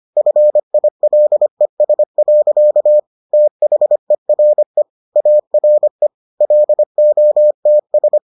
Morse code, heard as a keyed tone, about 25 wpm, 600 Hz. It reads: FILES. THERE ARE LOTS